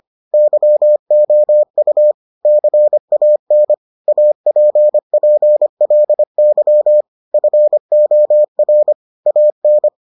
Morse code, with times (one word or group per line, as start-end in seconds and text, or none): 0.3-2.1 YOU
2.5-3.7 CAN
4.1-7.0 APPLY
7.3-8.9 FOR
9.3-9.9 AN